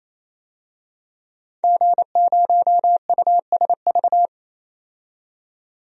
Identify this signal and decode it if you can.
Morse code, heard as a keyed tone, about 28 wpm, 700 Hz. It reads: G0USV
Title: DXCC-100 Call Signs - 5 Characters - 28wpm